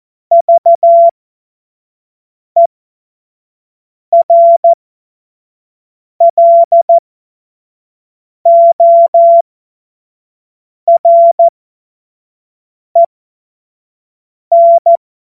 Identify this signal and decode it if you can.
Morse code, heard as a keyed tone, about 14 wpm, spaced as for 5 wpm, 680 Hz. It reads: VERLOREN
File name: Morsekurs / Witze / Witz 08. November 2022 25 F70